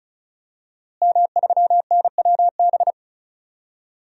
Morse code, heard as a keyed tone, about 34 wpm, 700 Hz. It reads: M3NWB
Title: DXCC-100 Call Signs - 5 Characters - Repeated 3x - 35wpm